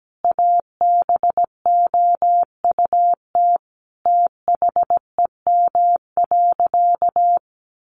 Morse code, heard as a keyed tone, about 17 wpm, 700 Hz. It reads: ABOUT THEM.
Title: Sentences from Top 800 Words 17wpm